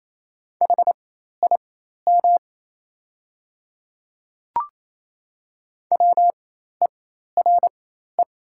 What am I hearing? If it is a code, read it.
Morse code, heard as a keyed tone, about 28 wpm, spaced as for 13 wpm, 700 Hz. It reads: HIM WERE
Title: 4x Character Spacing - Top 200 Words - 28wpm